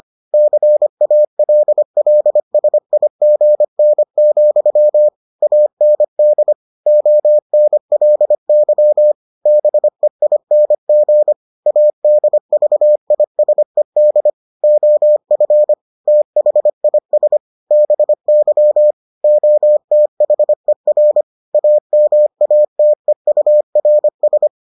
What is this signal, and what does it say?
Morse code, heard as a keyed tone, about 25 wpm, 600 Hz. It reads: CALLSIGN, AND ONLY BEING ADVISED OF THIS BY OTHER AMATEURS